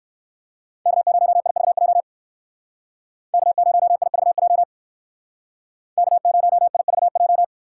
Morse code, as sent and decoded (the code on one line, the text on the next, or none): -.- ----- .. ...- .--- / -.- ----- .. ...- .--- / -.- ----- .. ...- .---
K0IVJ K0IVJ K0IVJ